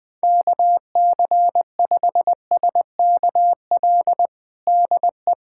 Morse code, heard as a keyed tone, about 20 wpm, 700 Hz. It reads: KC5SKL DE
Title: USA Callsign DE Callsign - 6 Characters - 20wpm